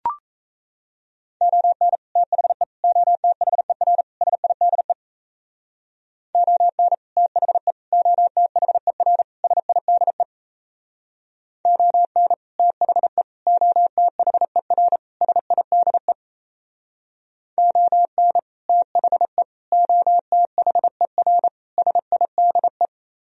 Morse code, heard as a keyed tone, about 37 wpm, 700 Hz. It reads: ON THE OTHER SIDE ON THE OTHER SIDE ON THE OTHER SIDE ON THE OTHER SIDE